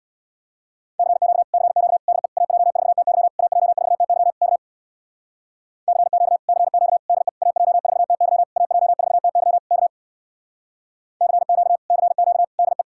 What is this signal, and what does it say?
Morse code, heard as a keyed tone, about 64 wpm, 700 Hz. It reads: CQ CQ DE AJ4TY AJ4TY K CQ CQ DE AJ4TY AJ4TY K CQ CQ DE